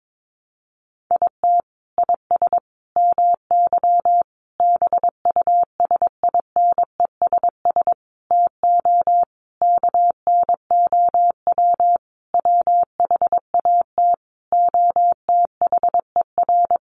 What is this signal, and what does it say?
Morse code, heard as a keyed tone, about 22 wpm, 700 Hz. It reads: IT IS MY BUSINESS TO KNOW WHAT OTHER
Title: Sentences from Top 600 Words 22wpm